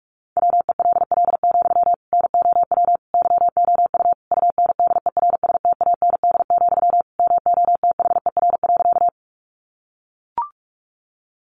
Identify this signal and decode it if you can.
Morse code, heard as a keyed tone, about 44 wpm, 700 Hz. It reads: WELL, NOW YOU UNDERSTAND, MOTHER.
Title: Sentences from Top 600 Words 45wpm